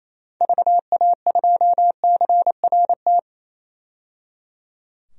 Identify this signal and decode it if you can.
Morse code, heard as a keyed tone, about 28 wpm, 700 Hz. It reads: VA2CRT
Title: DXCC-100 Call Signs - 6 Characters - 28wpm